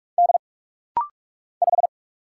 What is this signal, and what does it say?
Morse code, heard as a keyed tone, about 44 wpm, 700 Hz. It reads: D 5
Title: Single Letter-Number - Rapid-Fire 45wpm